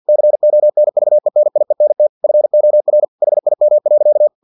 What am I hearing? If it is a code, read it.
Morse code, heard as a keyed tone, about 50 wpm, 600 Hz. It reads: CONVENIENT FOR HIM.
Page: Never 49wpm